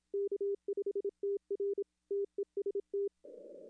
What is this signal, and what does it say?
Morse code, heard as a keyed tone, about 27 wpm, 410 Hz. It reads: K5TR TEST
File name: contests / arrl-160-k5tr-op-w5tn -0400